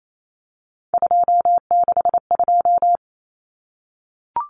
Morse code, heard as a keyed tone, about 28 wpm, 700 Hz. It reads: 262
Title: Numbers - 3-Digit 28wpm